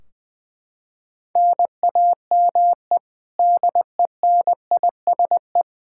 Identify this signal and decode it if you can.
Morse code, heard as a keyed tone, about 21 wpm, 700 Hz. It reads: NAME DENISE
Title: Practice QSO - Exchange 1 - Beginning - 20wpm